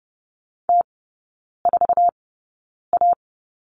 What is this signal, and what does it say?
Morse code, heard as a keyed tone, about 30 wpm, spaced as for 9 wpm, 700 Hz. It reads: T4A
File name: Lesson 18e - Letter m - Three Characters 30wpm